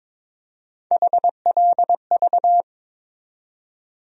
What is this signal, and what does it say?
Morse code, heard as a keyed tone, about 21 wpm, 700 Hz. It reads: HLV
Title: USA Call Sign Postfixes - 3 Letters 22wpm